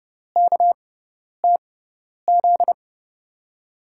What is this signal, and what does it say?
Morse code, heard as a keyed tone, about 31 wpm, spaced as for 10 wpm, 700 Hz. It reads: KTZ
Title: USA Call Signs - 6x Character Spacing - 30wpm